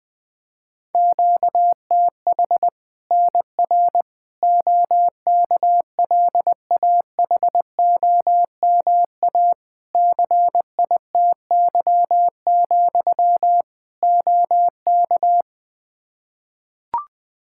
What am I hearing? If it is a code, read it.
Morse code, heard as a keyed tone, about 20 wpm, 700 Hz. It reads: QTH NR OKLAHOMA CITY, OK
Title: Practice QSO - Exchange 1 - Beginning - 20wpm